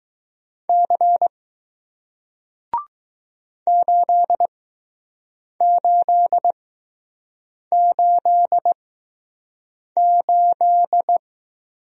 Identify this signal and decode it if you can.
Morse code, heard as a keyed tone, about 20 wpm, spaced as for 6 wpm, 700 Hz. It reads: C 8888